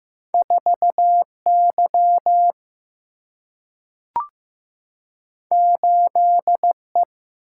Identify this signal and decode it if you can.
Morse code, heard as a keyed tone, about 16 wpm, 700 Hz. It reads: 4Y 8E
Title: Single-Letter Single-Number 15wpm